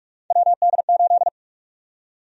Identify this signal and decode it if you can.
Morse code, heard as a keyed tone, about 46 wpm, 700 Hz. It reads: WD8
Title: USA Call Sign Prefixes - 45wpm